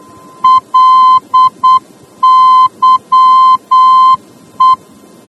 Morse code, heard as a keyed tone, about 8 wpm, 1030 Hz. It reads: LYE